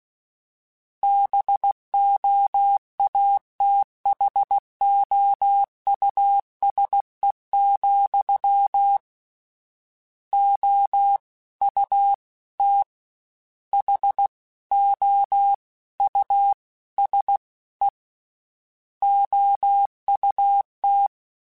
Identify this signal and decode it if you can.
Morse code, heard as a keyed tone, about 16 wpm, 800 Hz. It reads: BOATHOUSE, O U T H O U S E OUT
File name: General / General-mp3s / Houses @15